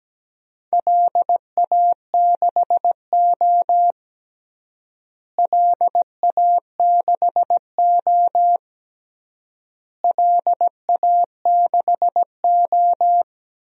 Morse code, heard as a keyed tone, about 17 wpm, 700 Hz. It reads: LA6O LA6O LA6O
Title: DXCC-100 Call Signs - 4 Characters - Repeated 3x - 17wpm